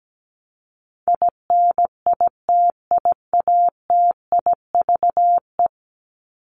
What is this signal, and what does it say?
Morse code, heard as a keyed tone, about 17 wpm, 700 Hz. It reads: INITIATIVE
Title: Top 1751-2000 Words 17wpm